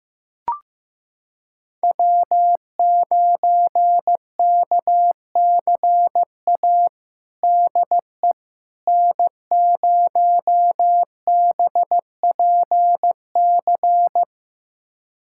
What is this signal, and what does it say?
Morse code, heard as a keyed tone, about 15 wpm, 700 Hz. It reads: W9KCA DE N0BPC
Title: USA Callsign DE Callsign - 5 Characters - 15wpm